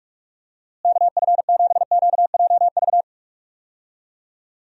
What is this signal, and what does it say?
Morse code, heard as a keyed tone, about 43 wpm, 700 Hz. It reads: KF7QJV
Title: USA Call Signs - 6 Characters - Repeated 3x - 45wpm